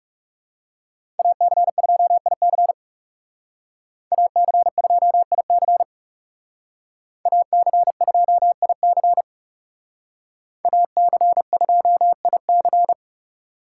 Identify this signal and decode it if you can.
Morse code, heard as a keyed tone, about 37 wpm, 700 Hz. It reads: AC2IC AC2IC AC2IC AC2IC
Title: Speed Racing - USA 2x2 Call Signs 30wpm